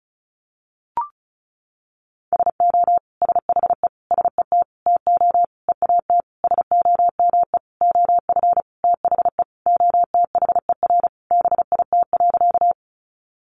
Morse code, heard as a keyed tone, about 34 wpm, 700 Hz. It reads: SO SHE SET TO EAT SOME OF THE OTHER BIT.
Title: Sentences from Top 900 Words 35wpm